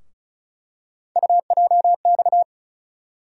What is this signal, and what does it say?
Morse code, heard as a keyed tone, about 36 wpm, 700 Hz. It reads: UJX
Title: USA Call Sign Postfixes - 3 Letters 35wpm